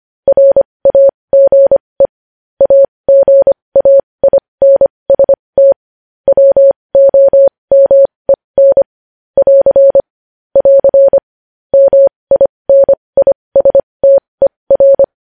Morse code, heard as a keyed tone, about 25 wpm, spaced as for 20 wpm, 550 Hz. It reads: RAGE AGAINST WOMEN <AR> <AR> MINISTER
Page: news-headlines-in-morse-code-at-20-wpm